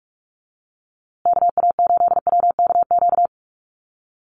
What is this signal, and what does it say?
Morse code, heard as a keyed tone, about 44 wpm, 700 Hz. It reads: KA8WKQ
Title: USA Call Signs Repeated 2x 45wpm